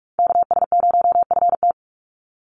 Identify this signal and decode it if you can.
Morse code, heard as a keyed tone, about 44 wpm, 700 Hz. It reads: KS0FT
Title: USA Call Signs 45wpm